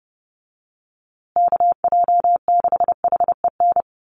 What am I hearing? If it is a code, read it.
Morse code, heard as a keyed tone, about 30 wpm, 700 Hz. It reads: KJ6HEN
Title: USA Call Signs Repeated 2x 30wpm